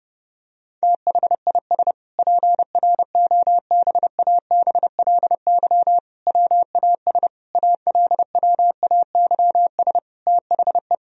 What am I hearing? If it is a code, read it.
Morse code, heard as a keyed tone, about 30 wpm, 700 Hz. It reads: THIS PROBABLY WAS ALWAYS THE
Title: Sentences from Top 1000 Words - Review Part 4 - 30wpm